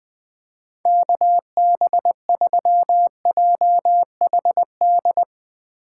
Morse code, heard as a keyed tone, about 20 wpm, 700 Hz. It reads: KB3JHD
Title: USA Call Signs - 6 Characters - 8x Character Spacing - 20wpm